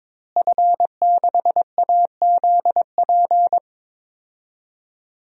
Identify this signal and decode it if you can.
Morse code, heard as a keyed tone, about 22 wpm, 700 Hz. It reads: F6AZP
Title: DXCC-100 Call Signs - 5 Characters - Repeated 2x - 22wpm